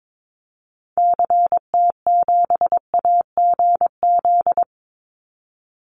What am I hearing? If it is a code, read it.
Morse code, heard as a keyed tone, about 22 wpm, 700 Hz. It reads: CT7AGZ
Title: Call Signs Repeated 2x - 22wpm